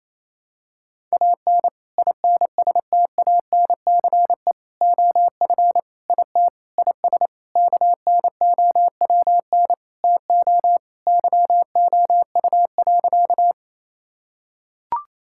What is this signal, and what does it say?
Morse code, heard as a keyed tone, about 27 wpm, 700 Hz. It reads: AN INSTANCE OF IT IS KNOWN TO YOU.
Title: Sentences from Top 1000 Words - Review Part 5 - 28wpm